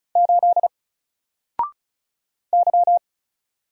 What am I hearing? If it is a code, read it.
Morse code, heard as a keyed tone, about 35 wpm, 700 Hz. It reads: 8 Y
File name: Single-Letter-Number-35wpm